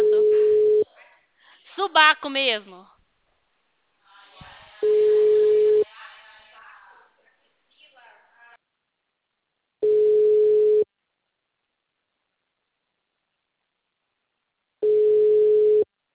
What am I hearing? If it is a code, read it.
Ringback tone (Europe, 425 Hz), with speech.